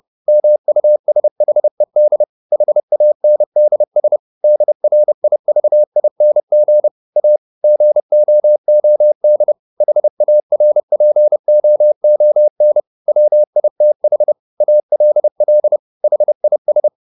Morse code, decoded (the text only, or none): MUSHED HANDS DRIVING A GOOD HARPOON WITH ALL HIS